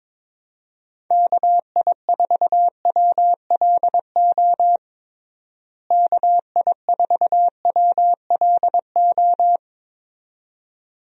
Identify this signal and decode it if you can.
Morse code, heard as a keyed tone, about 22 wpm, 700 Hz. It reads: KI4WLO KI4WLO
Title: USA Call Signs - 6 Characters - Repeated 2x - 22wpm